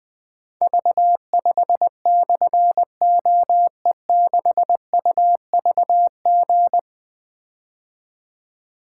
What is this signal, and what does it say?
Morse code, heard as a keyed tone, about 20 wpm, 700 Hz. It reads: V5/OE6UVG
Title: Difficult Call Signs 20wpm